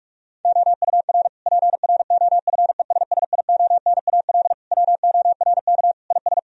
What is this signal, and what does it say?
Morse code, heard as a keyed tone, about 46 wpm, 700 Hz. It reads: OUR PROFESSIONAL WORK IS
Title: Sentences from Top 1000 Words - Review Part 4 - 45wpm